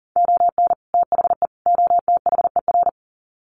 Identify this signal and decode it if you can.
Morse code, heard as a keyed tone, about 40 wpm, 700 Hz. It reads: ON THE OTHER
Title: Sets of 3 Words 40wpm